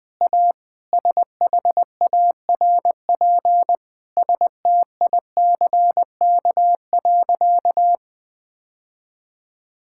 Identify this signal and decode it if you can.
Morse code, heard as a keyed tone, about 19 wpm, 700 Hz. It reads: A SHARP STICK.
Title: Sentences from Top 1600 Words 20wpm